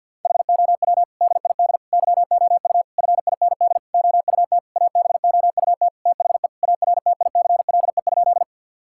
Morse code, heard as a keyed tone, about 50 wpm, 700 Hz. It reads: HOW DID YOU FIND OUT ABOUT THE ARTICLE?
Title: Sentences from Top 1000 Words - Review Part 5 - 50wpm